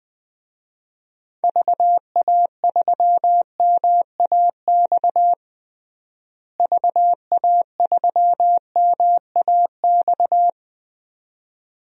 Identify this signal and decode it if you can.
Morse code, heard as a keyed tone, about 21 wpm, 700 Hz. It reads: VA3MAX VA3MAX
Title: DXCC-100 Call Signs - 6 Characters - Repeated 2x - 20wpm